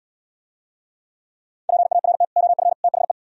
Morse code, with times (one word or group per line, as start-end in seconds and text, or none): 1.7-2.2 PART
2.4-2.7 OF
2.8-3.1 THE